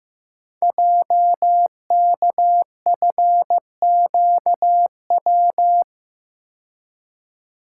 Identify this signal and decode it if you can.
Morse code, heard as a keyed tone, about 15 wpm, 700 Hz. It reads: JKFQW
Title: Code Groups - 5 Letters - 15wpm